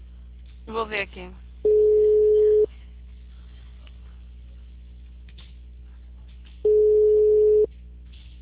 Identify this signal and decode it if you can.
Ringback tone (Europe, 425 Hz), with speech.